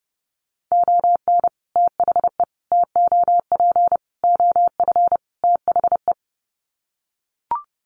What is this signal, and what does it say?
Morse code, heard as a keyed tone, about 30 wpm, 700 Hz. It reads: ON THE TOP OF THE
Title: Sets of 5 Words Encore 30wpm